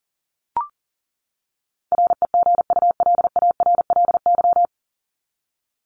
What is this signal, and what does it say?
Morse code, heard as a keyed tone, about 40 wpm, 700 Hz. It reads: REGULARLY
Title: Top 1251-1500 Words 40wpm